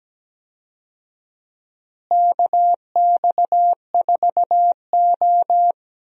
Morse code, heard as a keyed tone, about 17 wpm, 700 Hz. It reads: KX4O
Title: USA Call Signs - 4 Characters - Repeated 2x - 17wpm